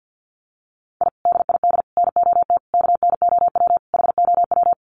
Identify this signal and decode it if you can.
Morse code, heard as a keyed tone, about 50 wpm, 700 Hz. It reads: I DID NOT KNOW HOW
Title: Sets of 5 Words Encore 50wpm